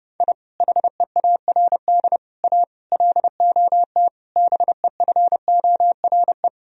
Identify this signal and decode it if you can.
Morse code, heard as a keyed tone, about 32 wpm, 700 Hz. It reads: I HEARD A LOT BEFORE
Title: Sentences from Top 1000 Words - Review Part 3 - 30wpm